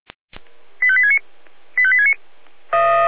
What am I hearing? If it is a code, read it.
Five-tone selective call (ZVEI), heard as quick runs of five tones, tones of about 70 ms. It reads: 85869 85869